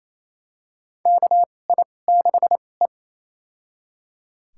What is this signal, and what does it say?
Morse code, heard as a keyed tone, about 28 wpm, spaced as for 20 wpm, 700 Hz. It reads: KI6E
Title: USA Call Signs - 4 Characters - 2x Character Spacing - 28wpm